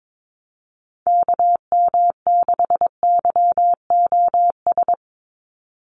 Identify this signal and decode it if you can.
Morse code, heard as a keyed tone, about 22 wpm, 700 Hz. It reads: KM6YOS
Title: USA Call Signs 22wpm